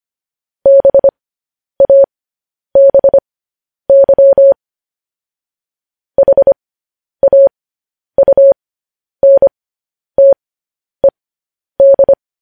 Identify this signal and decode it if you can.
Morse code, heard as a keyed tone, about 25 wpm, spaced as for 10 wpm, 550 Hz. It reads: BABY HAUNTED
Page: news headlines in morse code at 10 wpm